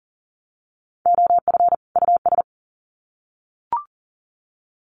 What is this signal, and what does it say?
Morse code, heard as a keyed tone, about 38 wpm, 700 Hz. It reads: OF US
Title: Sets of 2 Words 40wpm